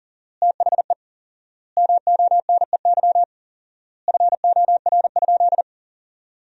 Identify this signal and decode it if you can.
Morse code, heard as a keyed tone, about 40 wpm, 700 Hz. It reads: THE MONEY FOR?
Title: 4x Word-Spacing - Sentences from Top 400 Words - 40wpm